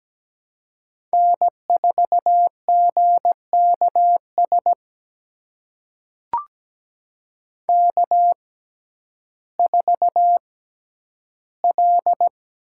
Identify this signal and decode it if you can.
Morse code, heard as a keyed tone, about 17 wpm, spaced as for 10 wpm, 700 Hz. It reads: N4GKS K4L
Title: USA Call Signs - 6x Character Spacing - 17wpm